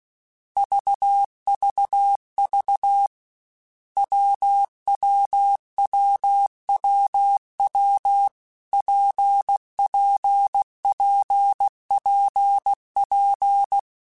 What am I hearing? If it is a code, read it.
Morse code, heard as a keyed tone, about 16 wpm, 800 Hz. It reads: VVV WWWWW PPPPP
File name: Difficult Combos / Difficult Combos-mp3s / WPJ WPJ1 @15